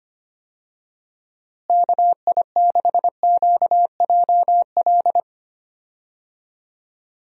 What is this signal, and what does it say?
Morse code, heard as a keyed tone, about 25 wpm, 700 Hz. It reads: KI6QJL
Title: USA Call Signs - 6 Characters - Repeated 2x - 25wpm